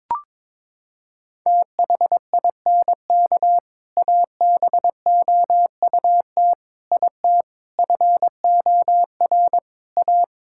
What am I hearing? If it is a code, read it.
Morse code, heard as a keyed tone, about 22 wpm, 700 Hz. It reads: THINK ABOUT IT FOR A
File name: practice / 22wpm / Sentences with 3-Word Phrases - 22wpm